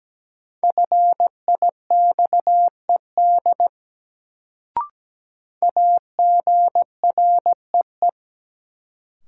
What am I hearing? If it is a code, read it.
Morse code, heard as a keyed tone, about 17 wpm, 700 Hz. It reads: FIXED AGREE
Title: Sentences from Top 1100 Words - New Word Drill Alt - 17wpm